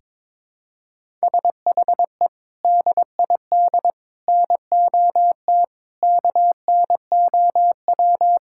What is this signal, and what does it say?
Morse code, heard as a keyed tone, about 22 wpm, 700 Hz. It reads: SHE DID NOT KNOW
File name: practice / 22wpm / Sentences with 3-Word Phrases - 22wpm